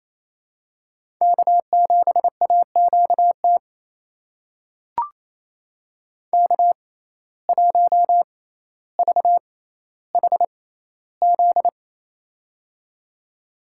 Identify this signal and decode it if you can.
Morse code, heard as a keyed tone, about 28 wpm, spaced as for 14 wpm, 700 Hz. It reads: K7AQT K1VHZ
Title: USA Call Signs - 5 Characters - 6x Character Spacing - 28wpm